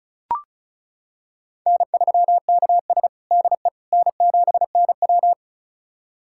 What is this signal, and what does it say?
Morse code, heard as a keyed tone, about 36 wpm, 700 Hz. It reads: N3KS DE N7NW
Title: USA Callsign DE Callsign - 4 Characters - 35wpm